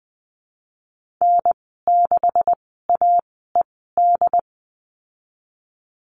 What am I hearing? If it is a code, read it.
Morse code, heard as a keyed tone, about 20 wpm, spaced as for 15 wpm, 700 Hz. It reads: N6AED
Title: 219 Lesson 35g - Number 0 - Call Signs 20wpm